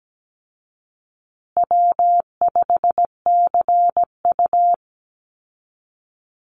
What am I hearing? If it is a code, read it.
Morse code, heard as a keyed tone, about 17 wpm, 700 Hz. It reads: W5CU
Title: Call Signs Repeated 2x - 17wpm